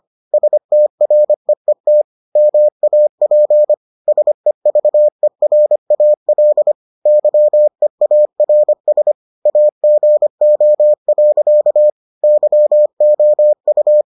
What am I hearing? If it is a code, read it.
Morse code, heard as a keyed tone, about 25 wpm, 600 Hz. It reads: STREET MAP SEVERAL YEARS AGO. YOU